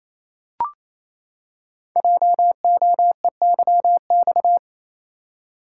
Morse code, heard as a keyed tone, about 28 wpm, 700 Hz. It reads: JOEYX
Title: Code Groups - 5 Letter:Numbers - 28wpm